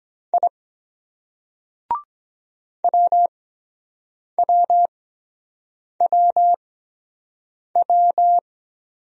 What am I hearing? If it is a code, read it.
Morse code, heard as a keyed tone, about 23 wpm, spaced as for 7 wpm, 700 Hz. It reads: I WWWW